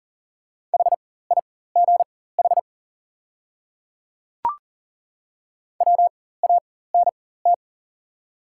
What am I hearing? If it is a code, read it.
Morse code, heard as a keyed tone, about 40 wpm, spaced as for 19 wpm, 690 Hz. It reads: HIGH WANT
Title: 4x Character Spacing - Top 200 Words - 40wpm